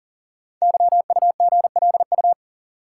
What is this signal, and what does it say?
Morse code, heard as a keyed tone, about 41 wpm, 690 Hz. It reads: YUGLU
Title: Code Groups - 5 Letter:Numbers - 40wpm